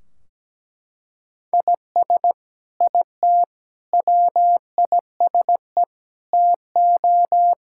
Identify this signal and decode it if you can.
Morse code, heard as a keyed tone, about 17 wpm, 700 Hz. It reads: IS IT WISE TO
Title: Sentences from Top 1600 Words 17wpm